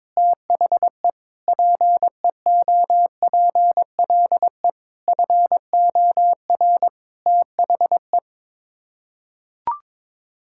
Morse code, heard as a keyed tone, about 22 wpm, 700 Hz. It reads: THE PEOPLE FOR THE
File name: Sets of 9 Words - 22wpm